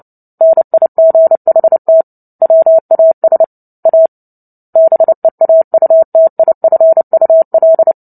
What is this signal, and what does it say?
Morse code, heard as a keyed tone, about 30 wpm, 650 Hz. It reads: NIGHT WAS A BEAUTIFUL
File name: Book Two / 03-Chapter 3 - 29 WPM